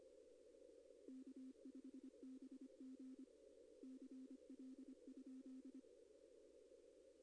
Morse code, heard as a keyed tone, about 25 wpm, 260 Hz. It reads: K5BG CL?
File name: contests / arrl-160-k5tr-op-w5tn -2200